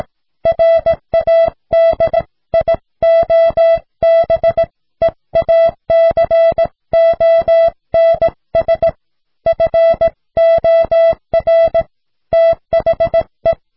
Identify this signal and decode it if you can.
Morse code, heard as a keyed tone, about 17 wpm, 660 Hz. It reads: RADIOBEACONS FOR THE